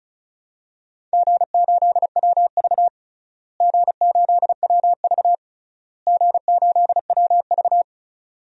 Morse code, heard as a keyed tone, about 36 wpm, 690 Hz. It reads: G8WV G8WV G8WV
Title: DXCC-100 Call Signs - 4 Characters - Repeated 3x - 35wpm